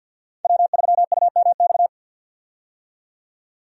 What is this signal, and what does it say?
Morse code, heard as a keyed tone, about 52 wpm, 700 Hz. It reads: W3UMX
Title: USA Call Signs - 5 Characters - 50wpm